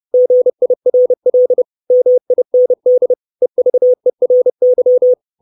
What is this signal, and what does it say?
Morse code, heard as a keyed tone, about 30 wpm, 500 Hz. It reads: GIRL MIND EVERY
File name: 632 Words 30WPM